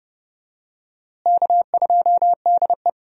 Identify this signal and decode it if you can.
Morse code, heard as a keyed tone, about 30 wpm, 700 Hz. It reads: K2DE
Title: USA Call Signs - 4 Characters - Repeated 2x - 30wpm